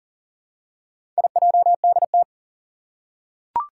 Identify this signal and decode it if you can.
Morse code, heard as a keyed tone, about 40 wpm, 690 Hz. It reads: IJDT
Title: Four Challenging Letter-Number Combinations 40wpm